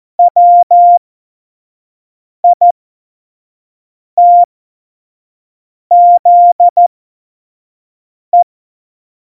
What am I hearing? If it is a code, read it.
Morse code, heard as a keyed tone, about 14 wpm, spaced as for 5 wpm, 700 Hz. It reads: WITZE
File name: Morsekurs / Witze / Witz 11 Juli 17, Tempo 25 F70